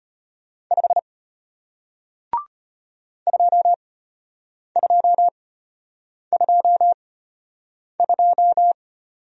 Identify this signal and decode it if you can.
Morse code, heard as a keyed tone, about 34 wpm, spaced as for 8 wpm, 700 Hz. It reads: 5 2222